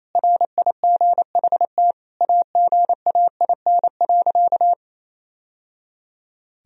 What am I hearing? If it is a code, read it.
Morse code, heard as a keyed tone, about 27 wpm, 700 Hz. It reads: RIGHT AGAIN.
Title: Sentences from Top 1000 Words - Review Part 1 - 28wpm